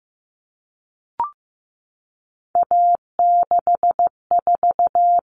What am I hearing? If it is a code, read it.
Morse code, heard as a keyed tone, about 15 wpm, 700 Hz. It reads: A64
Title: All Cut Numbers - 3-Digit 15wpm